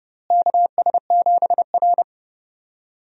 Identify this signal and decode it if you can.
Morse code, heard as a keyed tone, about 30 wpm, 700 Hz. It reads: KS7R
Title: USA Call Signs - 4 Characters - Repeated 3x - 30wpm